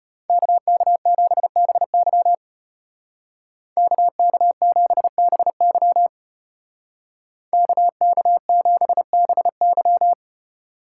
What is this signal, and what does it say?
Morse code, heard as a keyed tone, about 32 wpm, 700 Hz. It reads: KK7BY KK7BY KK7BY